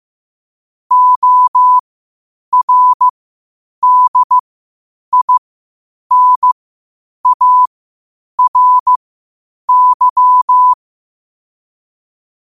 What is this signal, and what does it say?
Morse code, heard as a keyed tone, about 15 wpm, spaced as for 8 wpm, 1000 Hz. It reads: ORDINARY
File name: qotdfiles / 8wpm